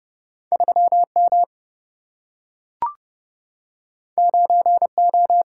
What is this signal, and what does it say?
Morse code, heard as a keyed tone, about 31 wpm, 700 Hz. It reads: 3M 9O
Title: Single-Letter Single-Number 30wpm